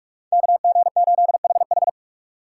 Morse code, heard as a keyed tone, about 46 wpm, 700 Hz. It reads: KG8HH
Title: USA Call Signs - 5 Characters - 6x Character Spacing - 45wpm